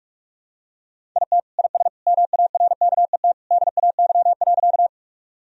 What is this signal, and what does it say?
Morse code, heard as a keyed tone, about 42 wpm, 700 Hz. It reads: IT IS MARKET DAY.
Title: Sentences from Top 1000 Words - Review Part 3 - 45wpm